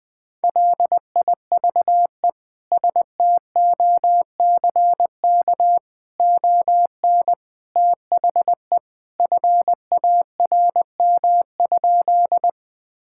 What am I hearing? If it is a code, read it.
Morse code, heard as a keyed tone, about 21 wpm, 700 Hz. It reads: LIVE STOCK ON THE FARM?
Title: Sentences from Top 1000 Words - Review Part 5 - 20wpm